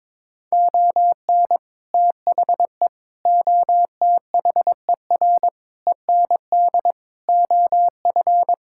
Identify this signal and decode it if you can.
Morse code, heard as a keyed tone, about 22 wpm, 700 Hz. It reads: ON THE OTHER END OF
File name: Sentences with 7-Word Phrases - 22wpm